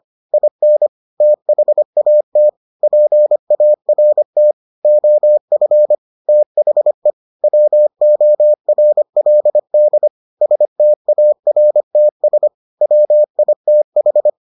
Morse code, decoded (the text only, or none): IN THAT PART OF THE WORLD STARTS WITH